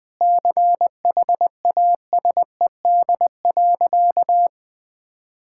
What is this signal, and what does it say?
Morse code, heard as a keyed tone, about 21 wpm, 700 Hz. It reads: CHASED.
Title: Sentences with 4-Word Phrases - 20wpm